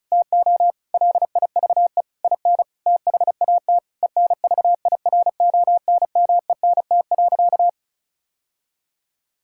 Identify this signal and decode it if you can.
Morse code, heard as a keyed tone, about 36 wpm, 700 Hz. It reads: TO LIVE IN THAT ENVIRONMENT.
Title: Sentences from Top 1000 Words - Review Part 5 - 35wpm